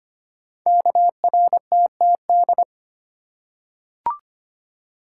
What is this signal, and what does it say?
Morse code, heard as a keyed tone, about 25 wpm, 700 Hz. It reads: KRTTD
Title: Code Groups - 5 Letter:Numbers - 25wpm